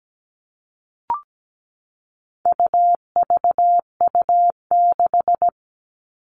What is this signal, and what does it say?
Morse code, heard as a keyed tone, about 17 wpm, 700 Hz. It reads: UVU6
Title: All Cut Numbers - 4-Digit 17wpm